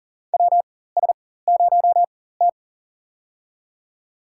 Morse code, heard as a keyed tone, about 37 wpm, spaced as for 18 wpm, 700 Hz. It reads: WS0T
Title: USA Call Signs - 4 Characters - 4x Character Spacing - 40wpm